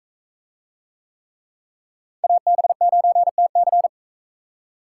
Morse code, heard as a keyed tone, about 39 wpm, 700 Hz. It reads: AB9TC